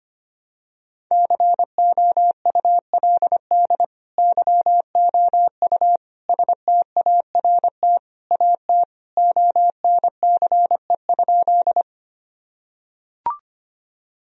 Morse code, heard as a keyed tone, about 25 wpm, 700 Hz. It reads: COULD YOU START AT ONCE?
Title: Sentences from Top 1000 Words - Review Part 1 - 25wpm